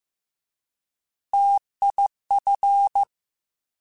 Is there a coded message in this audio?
Yes: Morse code from 1.3 to 3.0 s, heard as a keyed tone, about 15 wpm, 800 Hz. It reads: TIF